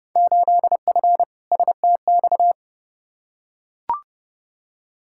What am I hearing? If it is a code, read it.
Morse code, heard as a keyed tone, about 30 wpm, 700 Hz. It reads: 8F STX
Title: Field Day Exchanges Encore - 30wpm